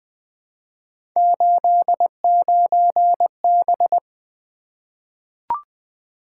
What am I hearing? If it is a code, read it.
Morse code, heard as a keyed tone, about 20 wpm, 700 Hz. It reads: 89B
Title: Three Letter-Number Combinations - 20wpm